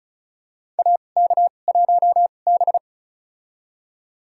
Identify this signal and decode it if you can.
Morse code, heard as a keyed tone, about 34 wpm, spaced as for 25 wpm, 700 Hz. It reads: AK1B
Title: USA Call Signs - 4 Characters - 2x Character Spacing - 35wpm